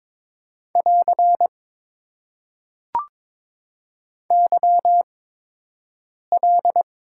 Morse code, heard as a keyed tone, about 22 wpm, spaced as for 6 wpm, 700 Hz. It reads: <AR> YL
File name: Top 90 QSO Elements - 8x Character Spacing - 22wpm